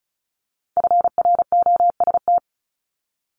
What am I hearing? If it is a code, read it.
Morse code, heard as a keyed tone, about 36 wpm, 700 Hz. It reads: FROST
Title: Multi-letter Phonemes 35wpm